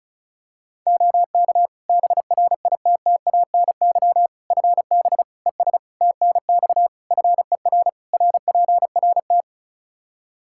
Morse code, heard as a keyed tone, about 36 wpm, 700 Hz. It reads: OK BRITTANY FB ES TNX FER RPRT
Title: K1USN SST QSO Practice 35wpm